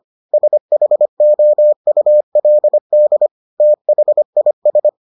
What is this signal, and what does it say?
Morse code, heard as a keyed tone, about 25 wpm, 600 Hz. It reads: SHOULD THIS